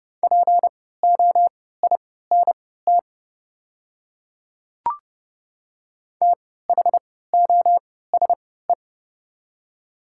Morse code, heard as a keyed tone, about 30 wpm, spaced as for 17 wpm, 700 Hz. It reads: POINT THOSE
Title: 3x Character Spacing - Top 500 Words - 30wpm